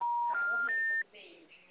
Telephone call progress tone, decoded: special information tone

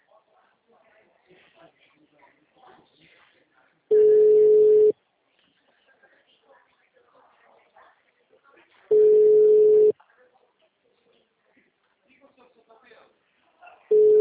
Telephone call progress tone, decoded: ringback tone

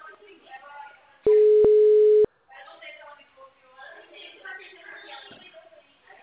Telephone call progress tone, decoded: ringback tone

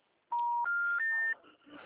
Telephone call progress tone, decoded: special information tone

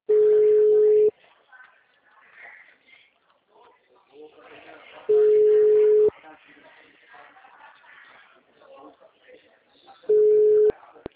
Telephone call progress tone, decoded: ringback tone

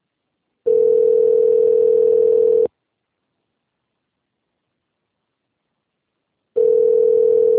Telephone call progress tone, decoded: ringback tone

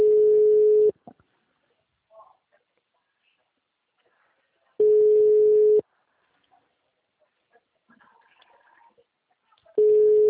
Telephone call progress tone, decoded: ringback tone

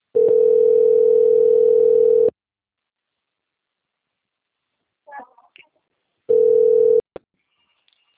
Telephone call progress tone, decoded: ringback tone

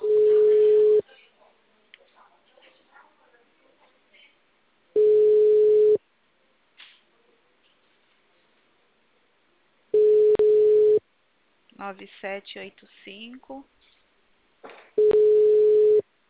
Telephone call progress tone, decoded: ringback tone

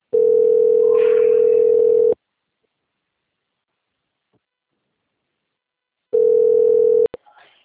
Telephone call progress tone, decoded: ringback tone

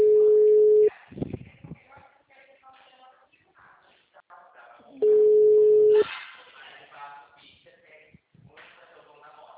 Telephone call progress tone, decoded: ringback tone

